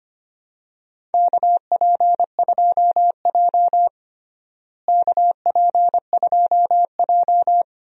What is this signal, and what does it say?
Morse code, heard as a keyed tone, about 25 wpm, 700 Hz. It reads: KP2J KP2J
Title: DXCC-100 Call Signs - 4 Characters - Repeated 2x - 25wpm